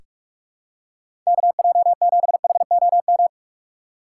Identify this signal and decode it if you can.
Morse code, heard as a keyed tone, about 46 wpm, 700 Hz. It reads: KJ7HOM
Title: USA Call Signs - 4x Character Spacing - 45wpm